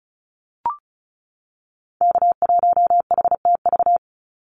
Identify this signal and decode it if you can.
Morse code, heard as a keyed tone, about 34 wpm, 700 Hz. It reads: K1HTV
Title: Call Signs - 35wpm